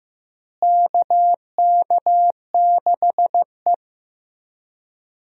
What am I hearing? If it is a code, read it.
Morse code, heard as a keyed tone, about 14 wpm, 700 Hz. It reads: KK6E
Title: USA Call Signs - 4 Characters - Repeated 3x - 15wpm